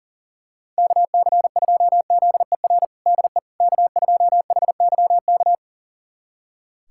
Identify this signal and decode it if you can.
Morse code, heard as a keyed tone, about 40 wpm, 700 Hz. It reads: KC2ZER DE K2HYK
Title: USA Callsign DE Callsign 40wpm